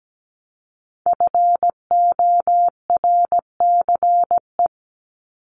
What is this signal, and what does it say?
Morse code, heard as a keyed tone, about 17 wpm, 700 Hz. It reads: FORCE
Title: ED Suffix - Top 100 Words with Base Words 17wpm